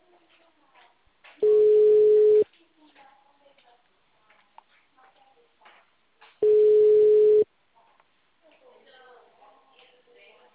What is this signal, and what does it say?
Ringback tone (Europe, 425 Hz), with speech.